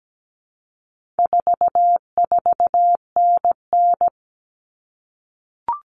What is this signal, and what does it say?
Morse code, heard as a keyed tone, about 17 wpm, 700 Hz. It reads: 44NN
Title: Common Cut Numbers - 4-Digit 17wpm